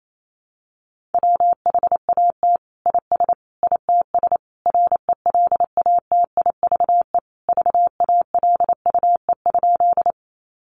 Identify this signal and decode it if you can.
Morse code, heard as a keyed tone, about 28 wpm, 700 Hz. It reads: WHAT IS ITS RELATIVE VALUE?
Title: Sentences from Top 1400 Words 28wpm